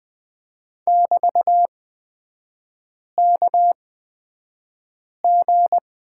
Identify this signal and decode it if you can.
Morse code, heard as a keyed tone, about 20 wpm, spaced as for 5 wpm, 700 Hz. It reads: <BT>KG
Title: 224 Lesson 36e - BT - Three Characters 20wpm